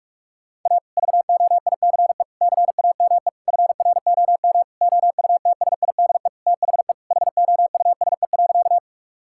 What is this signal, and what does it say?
Morse code, heard as a keyed tone, about 41 wpm, 700 Hz. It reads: A VOICE CAME FROM OUTSIDE THE HOUSE.
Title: Sentences from Top 1000 Words - Review Part 6 - 45wpm